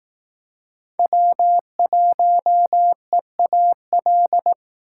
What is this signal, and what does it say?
Morse code, heard as a keyed tone, about 19 wpm, 700 Hz. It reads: W1EAL